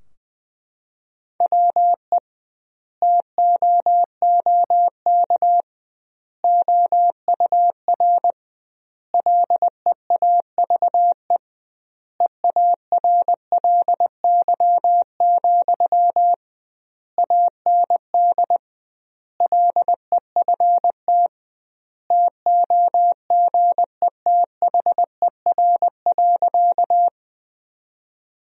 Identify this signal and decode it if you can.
Morse code, heard as a keyed tone, about 20 wpm, 700 Hz. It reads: WE TOOK OUR LEAVE EARLY, AND LEFT TOGETHER.
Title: 2x Word-Spacing - Sentences from Top 400 Words - 20wpm